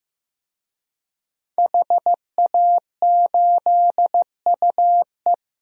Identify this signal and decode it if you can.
Morse code, heard as a keyed tone, about 15 wpm, 700 Hz. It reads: HA8UE
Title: DXCC-100 Call Signs - 5 Characters - Repeated 2x - 15wpm